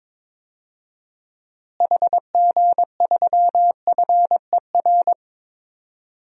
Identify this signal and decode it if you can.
Morse code, heard as a keyed tone, about 22 wpm, 700 Hz. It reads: HG3FER
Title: DXCC-100 Call Signs - 6 Characters - Repeated 2x - 22wpm